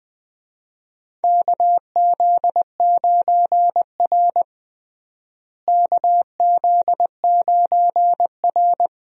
Morse code, heard as a keyed tone, about 20 wpm, 700 Hz. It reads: KZ9R KZ9R
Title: USA Call Signs - 4 Characters - Repeated 3x - 20wpm